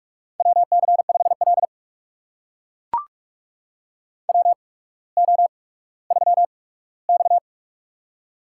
Morse code, heard as a keyed tone, about 44 wpm, spaced as for 19 wpm, 690 Hz. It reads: WC5L WO3X
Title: USA Call Signs - 4 Characters - 8x Character Spacing - 45wpm